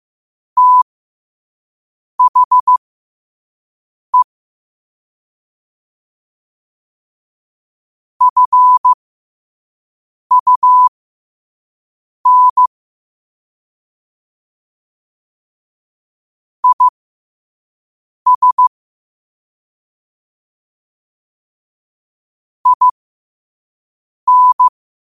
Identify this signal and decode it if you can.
Morse code, heard as a keyed tone, about 15 wpm, spaced as for 5 wpm, 1000 Hz. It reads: THE FUN IS IN
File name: qotdfiles / 5wpm